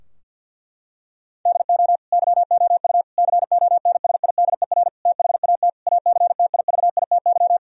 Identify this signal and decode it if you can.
Morse code, heard as a keyed tone, about 50 wpm, 700 Hz. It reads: DO YOU CONSIDER THAT ACTIVITY
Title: Sentences from Top 1000 Words - Review Part 4 - 50wpm